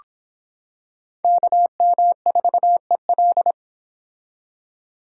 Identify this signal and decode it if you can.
Morse code, heard as a keyed tone, about 26 wpm, 700 Hz. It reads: KM4EL